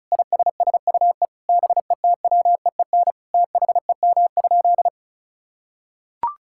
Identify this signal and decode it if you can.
Morse code, heard as a keyed tone, about 34 wpm, 700 Hz. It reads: ISSUE BETWEEN THEM?
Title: Sentences from Top 1000 Words - Review Part 6 - 35wpm